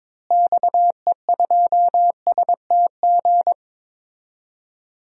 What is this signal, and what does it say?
Morse code, heard as a keyed tone, about 22 wpm, 700 Hz. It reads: XE2STG
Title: DXCC-100 Call Signs - 6 Characters - Repeated 2x - 22wpm